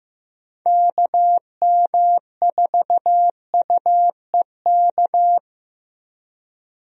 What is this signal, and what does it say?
Morse code, heard as a keyed tone, about 15 wpm, 700 Hz. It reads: KM4UEK
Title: USA Call Signs - 6 Characters - Repeated 3x - 15wpm